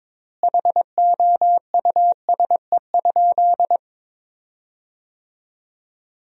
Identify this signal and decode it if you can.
Morse code, heard as a keyed tone, about 22 wpm, 700 Hz. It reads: HOUSE?
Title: 3x Word-Spacing - Sentences from Top 200 Words - 22wpm